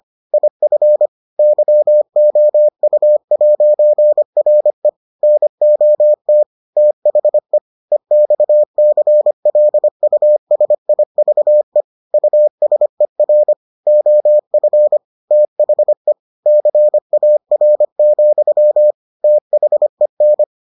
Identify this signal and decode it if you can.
Morse code, heard as a keyed tone, about 25 wpm, 600 Hz. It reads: IF YOU'RE NOT THE EXCLUSIVE USER OF THE CAR, THEN